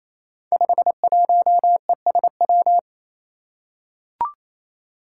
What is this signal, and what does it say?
Morse code, heard as a keyed tone, about 28 wpm, 700 Hz. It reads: 51ESW
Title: Five Challenging Letter-Number Combinations 28wpm